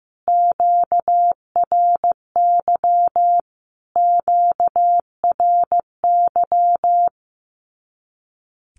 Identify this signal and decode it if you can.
Morse code, heard as a keyed tone, about 15 wpm, 700 Hz. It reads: QRY QRY
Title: Q-Codes Repeated 3x 15wpm